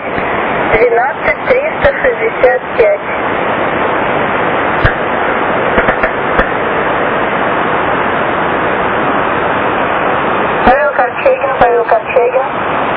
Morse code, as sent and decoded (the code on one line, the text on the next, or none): none